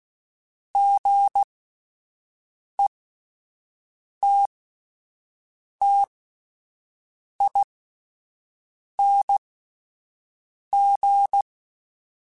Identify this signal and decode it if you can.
Morse code, heard as a keyed tone, about 16 wpm, spaced as for 5 wpm, 800 Hz. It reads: GETTING
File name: Misc / Misc-mp3s / Gotten @10